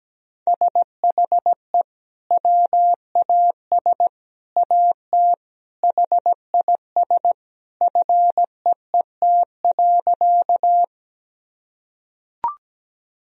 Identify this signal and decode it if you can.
Morse code, heard as a keyed tone, about 17 wpm, 700 Hz. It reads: SHE WAS AT HIS FEET.
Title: Sentences from Top 1000 Words - Review Part 1 - 17wpm